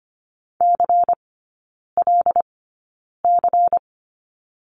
Morse code, heard as a keyed tone, about 25 wpm, spaced as for 9 wpm, 700 Hz. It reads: CLC
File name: course / 25:10wpm / 093 Lesson 16e - Letter c - Three Characters 25wpm